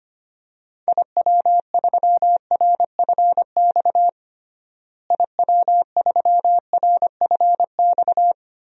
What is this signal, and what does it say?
Morse code, heard as a keyed tone, about 25 wpm, 700 Hz. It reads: IW3RFX IW3RFX
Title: DXCC-100 Call Signs - 6 Characters - Repeated 2x - 25wpm